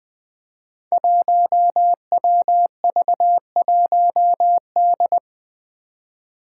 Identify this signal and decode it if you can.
Morse code, heard as a keyed tone, about 20 wpm, 700 Hz. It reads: 1WV1D
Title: Five Challenging Letter-Number Combinations 20wpm